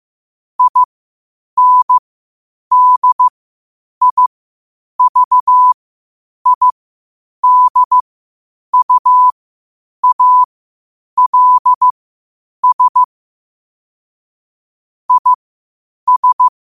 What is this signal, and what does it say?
Morse code, heard as a keyed tone, about 15 wpm, spaced as for 8 wpm, 1000 Hz. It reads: INDIVIDUALS IS
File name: qotdfiles / 8wpm